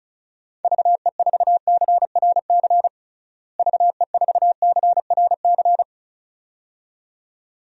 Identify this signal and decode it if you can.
Morse code, heard as a keyed tone, about 36 wpm, 700 Hz. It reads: VE4CRC VE4CRC
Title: DXCC-100 Call Signs - 6 Characters - Repeated 2x - 35wpm